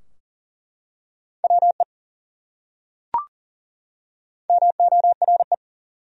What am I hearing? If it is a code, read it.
Morse code, heard as a keyed tone, about 40 wpm, 700 Hz. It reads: WE MORE
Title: Top 100 Words 40wpm